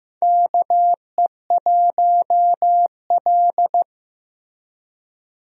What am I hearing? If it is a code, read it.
Morse code, heard as a keyed tone, about 15 wpm, 700 Hz. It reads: KE1L
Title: USA Call Signs - 4 Characters - Repeated 2x - 15wpm